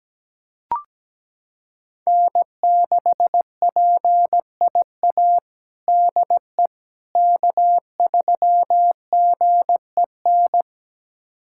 Morse code, heard as a keyed tone, about 17 wpm, 700 Hz. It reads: N6PIA DE K3GEN
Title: USA Callsign DE Callsign - 5 Characters - 17wpm